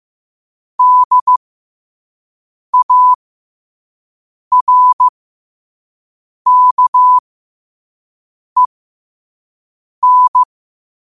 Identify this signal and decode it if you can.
Morse code, heard as a keyed tone, about 15 wpm, spaced as for 5 wpm, 1000 Hz. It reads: DARKEN
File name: qotdfiles / 5wpm